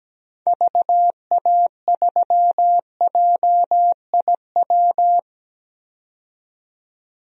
Morse code, heard as a keyed tone, about 17 wpm, 700 Hz. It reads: VA3JIW
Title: DXCC-100 Call Signs - 6 Characters - 17wpm